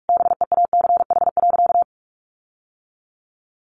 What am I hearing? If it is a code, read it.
Morse code, heard as a keyed tone, about 44 wpm, 700 Hz. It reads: BEACH.
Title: Sentences from Top 1500 Words 45wpm